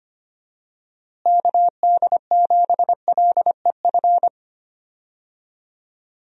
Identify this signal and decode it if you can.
Morse code, heard as a keyed tone, about 25 wpm, 700 Hz. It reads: KD7LEF
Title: USA Call Signs - 6 Characters - Repeated 2x - 25wpm